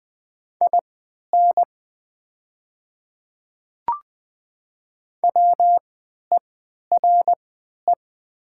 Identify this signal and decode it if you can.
Morse code, heard as a keyed tone, about 20 wpm, spaced as for 11 wpm, 700 Hz. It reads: IN WERE
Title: 3x Character Spacing - Top 300 Words - 20wpm